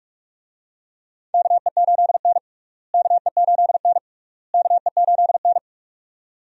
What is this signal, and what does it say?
Morse code, heard as a keyed tone, about 42 wpm, 700 Hz. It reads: KE8N KE8N KE8N
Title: USA Call Signs - 4 Characters - Repeated 3x - 45wpm